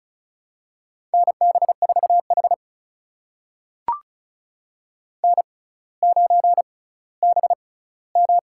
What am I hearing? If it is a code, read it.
Morse code, heard as a keyed tone, about 34 wpm, spaced as for 18 wpm, 700 Hz. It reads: NB4H N9BM
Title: USA Call Signs - 4 Characters - 6x Character Spacing - 35wpm